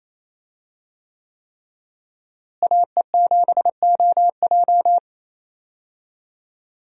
Morse code, heard as a keyed tone, about 27 wpm, 700 Hz. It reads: AE7OJ
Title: USA Call Signs - 5 Characters - Repeated 3x - 28wpm